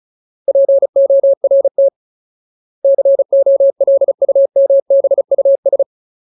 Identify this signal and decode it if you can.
Morse code, heard as a keyed tone, about 34 wpm, 550 Hz. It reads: PORT COLUMBUS
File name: Ultimate Word List / 35WPM Ultimate